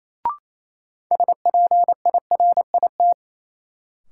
Morse code, heard as a keyed tone, about 28 wpm, 700 Hz. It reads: SPIRIT
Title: Sentences from Top 1400 Words - New Word Drill Alt - 28wpm